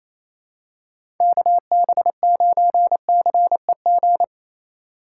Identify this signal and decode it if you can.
Morse code, heard as a keyed tone, about 28 wpm, 700 Hz. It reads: KB9CEG
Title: USA Call Signs - 6 Characters - 4x Character Spacing - 28wpm